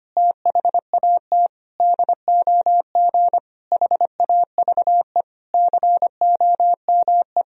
Morse code, heard as a keyed tone, about 25 wpm, 700 Hz. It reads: THAT DOG HAVE COME
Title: Sentences from Top 1000 Words - Review Part 2 - 25wpm